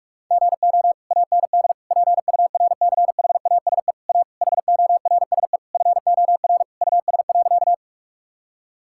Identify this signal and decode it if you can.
Morse code, heard as a keyed tone, about 46 wpm, 700 Hz. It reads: GO AND PURCHASE A HORSE FOR US.
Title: Sentences from Top 1000 Words - Review Part 5 - 45wpm